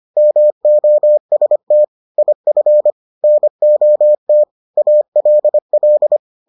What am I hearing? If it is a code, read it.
Morse code, heard as a keyed tone, about 25 wpm, 600 Hz. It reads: MOST IF NOT ALL